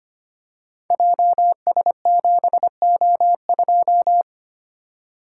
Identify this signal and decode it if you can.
Morse code, heard as a keyed tone, about 25 wpm, 700 Hz. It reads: JS7O2
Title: Code Groups - 5 Letter:Numbers - 25wpm